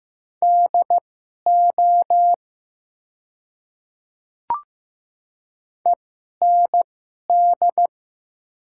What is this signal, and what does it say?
Morse code, heard as a keyed tone, about 15 wpm, spaced as for 11 wpm, 700 Hz. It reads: DO END
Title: 2x Character Spacing - Top 200 Words - 15wpm